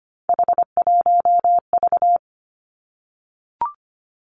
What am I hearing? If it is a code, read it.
Morse code, heard as a keyed tone, about 25 wpm, 700 Hz. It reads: H1V
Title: Three Challenging Letter-Number Combinations 25wpm